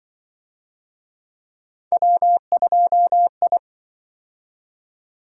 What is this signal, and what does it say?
Morse code, heard as a keyed tone, about 25 wpm, 700 Hz. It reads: W2I